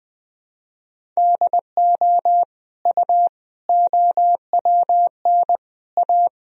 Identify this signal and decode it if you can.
Morse code, heard as a keyed tone, about 21 wpm, 700 Hz. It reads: DO U OWN A
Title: Sentences from Top 400 Words and Abbreviations 20wpm